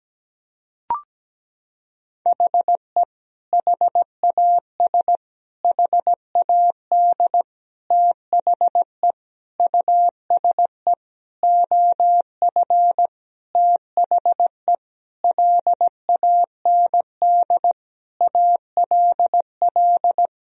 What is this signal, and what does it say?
Morse code, heard as a keyed tone, about 17 wpm, 700 Hz. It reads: HE HAS HAD THE USE OF THE LAND ALL
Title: Sentences from Top 1000 Words - Review Part 1 - 17wpm